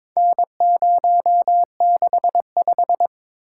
Morse code, heard as a keyed tone, about 22 wpm, 700 Hz. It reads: N065
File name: practice / 22wpm / Four Letter-Number Combinations - 22wpm